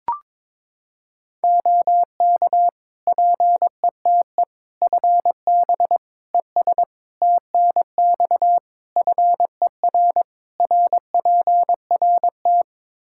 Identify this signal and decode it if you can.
Morse code, heard as a keyed tone, about 22 wpm, 700 Hz. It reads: OK PETE FB ES TNX FER RPRT
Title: Practice QSO - Exchange 2 - Info - 22wpm